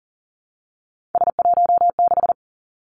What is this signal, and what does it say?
Morse code, heard as a keyed tone, about 40 wpm, 700 Hz. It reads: S16
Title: Three Challenging Letter-Number Combinations 40wpm